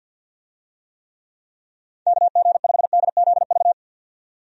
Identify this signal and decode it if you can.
Morse code, heard as a keyed tone, about 47 wpm, 700 Hz. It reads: KG5DZV
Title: USA Call Signs - 6 Characters - 50wpm